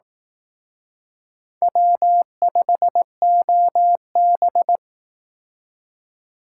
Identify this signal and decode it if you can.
Morse code, heard as a keyed tone, about 19 wpm, 700 Hz. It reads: W5OB
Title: Speed Racing - USA 1x2 Call Signs 15wpm